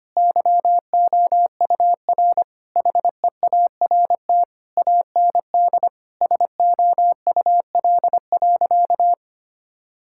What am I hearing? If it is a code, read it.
Morse code, heard as a keyed tone, about 25 wpm, 700 Hz. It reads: YOUR HEART AND SOUL.
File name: practice / 25wpm / Sentences with 9-Word Phrases - 25wpm